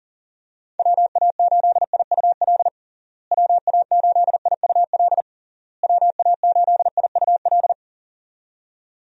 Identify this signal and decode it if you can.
Morse code, heard as a keyed tone, about 40 wpm, 700 Hz. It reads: WA8IUL WA8IUL WA8IUL
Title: USA Call Signs - 6 Characters - Repeated 3x - 40wpm